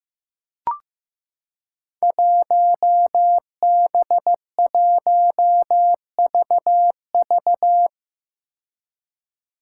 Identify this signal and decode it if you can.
Morse code, heard as a keyed tone, about 15 wpm, 700 Hz. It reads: 1B1VV
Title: Five Challenging Letter-Number Combinations 15wpm